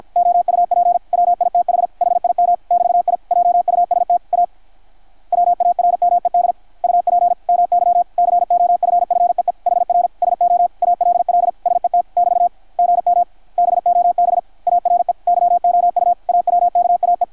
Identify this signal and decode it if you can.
Morse code, heard as a keyed tone, about 50 wpm, 660 Hz, with radio noise. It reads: OUP WITH HIM <BT> I JUST A WARMED UP MY COFFEE HR SO ALL SET <BT> GM BOB ARE YOU AWKAE
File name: archived / CRSnet